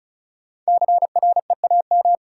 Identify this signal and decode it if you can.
Morse code, heard as a keyed tone, about 36 wpm, 700 Hz. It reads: CREAM
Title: Sentences from Top 1500 Words - New Word Drill - 35wpm